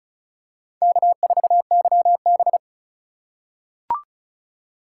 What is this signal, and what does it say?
Morse code, heard as a keyed tone, about 36 wpm, 700 Hz. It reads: K4YB
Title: USA Call Signs - 4 Characters - 3x Character Spacing - 35wpm